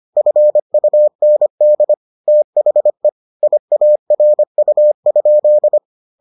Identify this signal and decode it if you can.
Morse code, heard as a keyed tone, about 25 wpm, 600 Hz. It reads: FUND THE IARU?